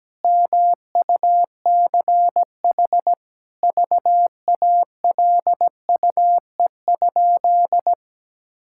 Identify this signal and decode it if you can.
Morse code, heard as a keyed tone, about 17 wpm, 700 Hz. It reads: MUCH VALUE?
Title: Sentences from Top 1000 Words - Review Part 6 - 17wpm